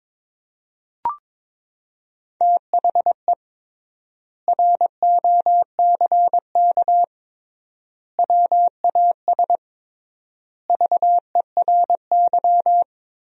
Morse code, heard as a keyed tone, about 22 wpm, 700 Hz. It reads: THE ROCK WAS VERY
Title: 3x Word-Spacing - Sentences from Top 400 Words - 22wpm